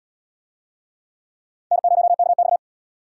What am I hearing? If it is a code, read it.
Morse code, heard as a keyed tone, about 65 wpm, 700 Hz. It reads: N0KX